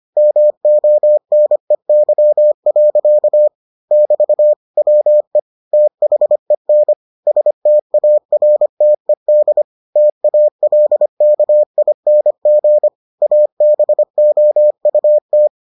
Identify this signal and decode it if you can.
Morse code, heard as a keyed tone, about 25 wpm, 600 Hz. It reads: MONEY. <BT> WE THEN STARTED TALKING ABOUT